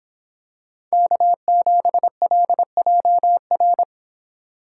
Morse code, heard as a keyed tone, about 26 wpm, 700 Hz. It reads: K7LJR